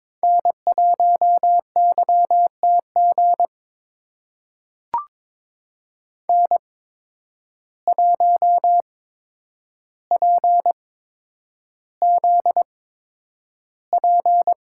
Morse code, heard as a keyed tone, about 22 wpm, spaced as for 9 wpm, 700 Hz. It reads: N1YTG N1PZP
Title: USA Call Signs - 5 Characters - 8x Character Spacing - 22wpm